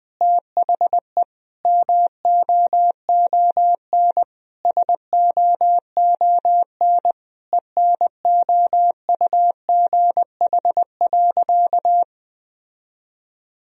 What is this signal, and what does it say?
Morse code, heard as a keyed tone, about 21 wpm, 700 Hz. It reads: THE MOON SOON ENOUGH.
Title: Sentences from Top 1000 Words - Review Part 3 - 20wpm